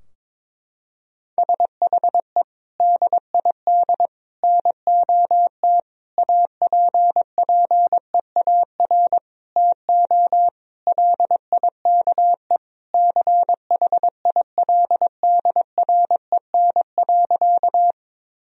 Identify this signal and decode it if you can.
Morse code, heard as a keyed tone, about 22 wpm, 700 Hz. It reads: SHE DID NOT APPEAR TO LIKE CHILDREN.
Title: Sentences from Top 1000 Words - Review Part 2 - 22wpm